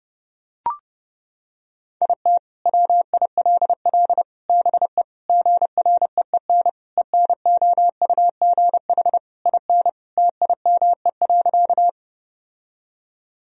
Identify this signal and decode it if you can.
Morse code, heard as a keyed tone, about 30 wpm, 700 Hz. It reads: IT WILL BE GREEN ENOUGH IN TIME.
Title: Sentences from Top 1000 Words - Review Part 2 - 30wpm